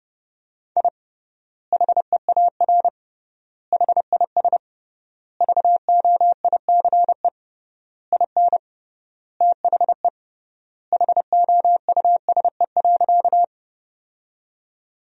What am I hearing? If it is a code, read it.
Morse code, heard as a keyed tone, about 32 wpm, 700 Hz. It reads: I HEAR HIS VOICE IN THE HOUSE.
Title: 3x Word-Spacing - Sentences from Top 500 Words - 30wpm